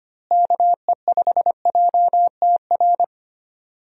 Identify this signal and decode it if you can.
Morse code, heard as a keyed tone, about 25 wpm, 700 Hz. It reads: KE5JTR
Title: USA Call Signs - 6 Characters - Repeated 3x - 25wpm